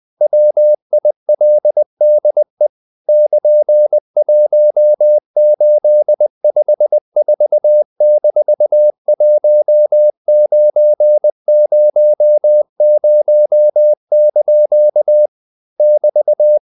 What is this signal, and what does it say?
Morse code, heard as a keyed tone, about 20 wpm, 600 Hz. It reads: WILDE <KN>1854-1900) <BT>
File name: CW / Practice / EnglishQuotations0320